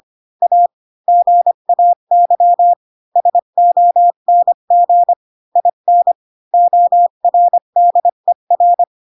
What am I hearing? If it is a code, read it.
Morse code, heard as a keyed tone, about 25 wpm, 700 Hz. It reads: A GAY SONG IN ORDER
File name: Pinocchio / Pinocchio ch1 20 25 ik1hge